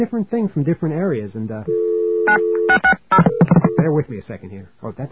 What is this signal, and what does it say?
DTMF, heard as telephone touch tones, tones of about 90 ms, with speech. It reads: 9660